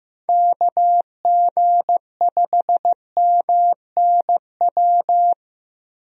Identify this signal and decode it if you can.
Morse code, heard as a keyed tone, about 15 wpm, 700 Hz. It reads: KG5MNW
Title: USA Call Signs - 6 Characters - 15wpm